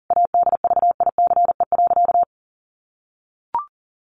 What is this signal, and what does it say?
Morse code, heard as a keyed tone, about 40 wpm, 700 Hz. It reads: ADVICE.
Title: Sentences from Top 1100 Words 40wpm